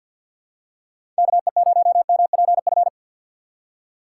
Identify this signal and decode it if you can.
Morse code, heard as a keyed tone, about 51 wpm, 700 Hz. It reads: KE0MPF
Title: USA Call Signs - 6 Characters - 3x Character Spacing - 50wpm